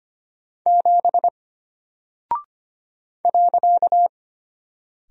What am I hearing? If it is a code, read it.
Morse code, heard as a keyed tone, about 25 wpm, 700 Hz. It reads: 7 .
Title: Single Character (Letter:Number:Punctuation:Prosign) Alt 25wpm